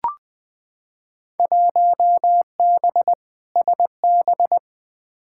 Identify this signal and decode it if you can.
Morse code, heard as a keyed tone, about 20 wpm, 700 Hz. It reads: 1B SB
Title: Field Day Exchanges - 20wpm